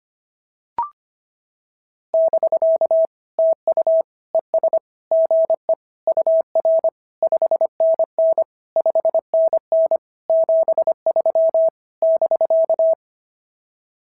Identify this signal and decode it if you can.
Morse code, heard as a keyed tone, about 22 wpm, 650 Hz. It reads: <BK> TU ES GE UR 5NN 5NN 73 <BK>
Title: SOTA QSO Practice 25wpm